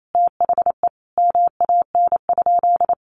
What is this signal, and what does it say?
Morse code, heard as a keyed tone, about 28 wpm, 700 Hz. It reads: THE MAN?
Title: Sentences from Top 1300 Words 28wpm